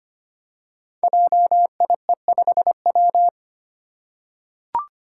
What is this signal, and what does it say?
Morse code, heard as a keyed tone, about 25 wpm, 700 Hz. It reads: JIE5W
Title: Code Groups - 5 Letter:Numbers - 25wpm